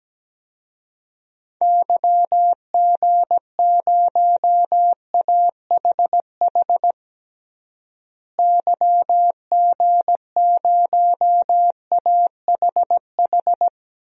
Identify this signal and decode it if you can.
Morse code, heard as a keyed tone, about 17 wpm, 700 Hz. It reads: YG0AHH YG0AHH
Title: DXCC-100 Call Signs - 6 Characters - Repeated 3x - 17wpm